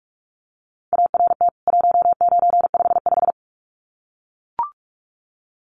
Morse code, heard as a keyed tone, about 44 wpm, 700 Hz. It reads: ART 1955
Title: CWT Exchange 45wpm